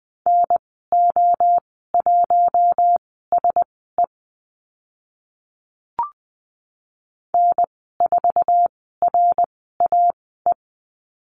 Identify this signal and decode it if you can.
Morse code, heard as a keyed tone, about 20 wpm, spaced as for 14 wpm, 700 Hz. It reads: NO1SE N4RAE
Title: 062 Lesson 11g - Letter h - Call Signs 20wpm